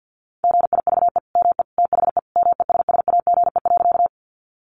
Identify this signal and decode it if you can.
Morse code, heard as a keyed tone, about 50 wpm, 700 Hz. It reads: GIVE ME THE MESSAGE.
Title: Sentences from Top 1100 Words 50wpm